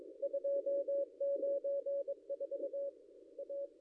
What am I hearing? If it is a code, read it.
Morse code, heard as a keyed tone, about 23 wpm, 560 Hz. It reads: 294 A
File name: contests / ss-cw-k5tr-op-n5rz -0800